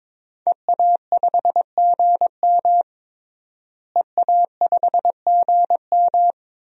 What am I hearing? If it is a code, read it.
Morse code, heard as a keyed tone, about 22 wpm, 700 Hz. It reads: EA5GM EA5GM
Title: DXCC-100 Call Signs - 5 Characters - Repeated 2x - 22wpm